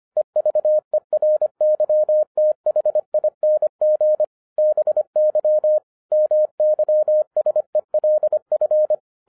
Morse code, heard as a keyed tone, about 25 wpm, 610 Hz. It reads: EVERYTHING BY MYSELF
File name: 2011-10-mp3 / n04-JUST MOVED 25wpm